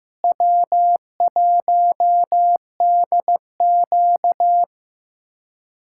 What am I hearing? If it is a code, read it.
Morse code, heard as a keyed tone, about 15 wpm, 700 Hz. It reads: W1DQ
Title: USA Callsign DE Callsign - 4 Characters - 15wpm